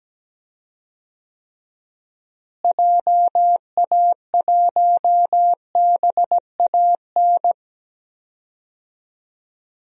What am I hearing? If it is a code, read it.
Morse code, heard as a keyed tone, about 17 wpm, 700 Hz. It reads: JA1BAN
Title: DXCC-100 Call Signs - 6 Characters - Repeated 2x - 17wpm